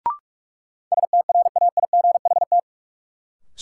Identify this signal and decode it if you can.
Morse code, heard as a keyed tone, about 44 wpm, 700 Hz. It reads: STRAIGHT
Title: Sentences from Top 1000 Words - New Word Drill - 45wpm